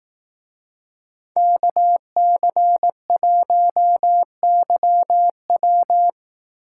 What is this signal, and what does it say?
Morse code, heard as a keyed tone, about 18 wpm, 700 Hz. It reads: KC1YW
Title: Speed Racing - USA Call Signs 15wpm